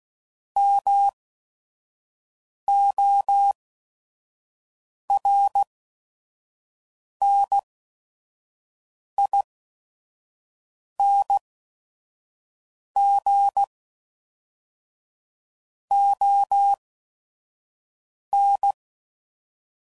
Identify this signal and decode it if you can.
Morse code, heard as a keyed tone, about 16 wpm, spaced as for 5 wpm, 800 Hz. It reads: MORNING ON
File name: CW Code Groups @5WPM / Weather Sets / Weather 4 24